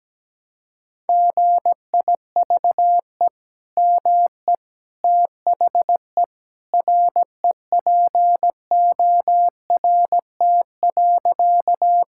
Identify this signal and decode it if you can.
Morse code, heard as a keyed tone, about 17 wpm, 700 Hz. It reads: GIVE ME THE REPORT.
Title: Sentences from Top 1000 Words - Review Part 6 - 17wpm